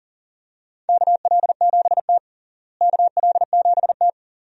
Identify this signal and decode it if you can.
Morse code, heard as a keyed tone, about 40 wpm, 700 Hz. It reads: KL7T KL7T
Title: DXCC-100 Call Signs - 4 Characters - Repeated 2x - 40wpm